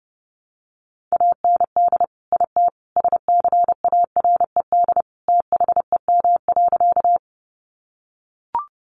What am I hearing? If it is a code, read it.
Morse code, heard as a keyed tone, about 30 wpm, 700 Hz. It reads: AND IT SCARED THEM.
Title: Sentences from Top 700 Words 30wpm